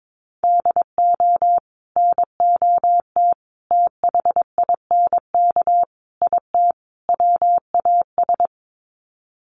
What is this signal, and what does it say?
Morse code, heard as a keyed tone, about 22 wpm, 700 Hz. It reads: DO NOT THINK IT WAS
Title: Sets of 5 Words Encore 22wpm